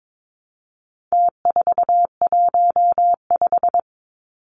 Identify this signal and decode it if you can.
Morse code, heard as a keyed tone, about 22 wpm, 700 Hz. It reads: T415
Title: Numbers - Contest Style - 4-Digit 22wpm